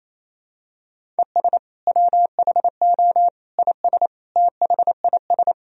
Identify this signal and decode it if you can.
Morse code, heard as a keyed tone, about 28 wpm, 700 Hz. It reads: ES WHO IS THIS
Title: Sentences from Top 300 Words and Abbreviations 28wpm